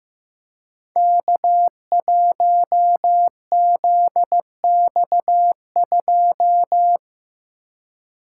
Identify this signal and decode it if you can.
Morse code, heard as a keyed tone, about 15 wpm, 700 Hz. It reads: K1ZX2
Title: Code Groups - 5 Letter:Numbers - 15wpm